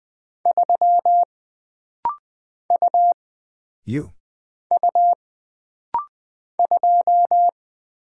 Morse code, heard as a keyed tone, about 20 wpm, 700 Hz. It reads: 3 U U 2
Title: Single Character (Letter:Number:Punctuation:Prosign) - Rapid-Fire - 20wpm